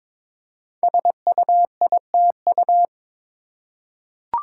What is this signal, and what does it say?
Morse code, heard as a keyed tone, about 22 wpm, 700 Hz. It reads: SUITU
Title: Code Groups - 5 Letters - 22wpm